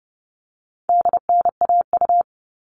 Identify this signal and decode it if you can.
Morse code, heard as a keyed tone, about 30 wpm, 700 Hz. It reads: DNAU
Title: All Cut Numbers - 4-Digit 30wpm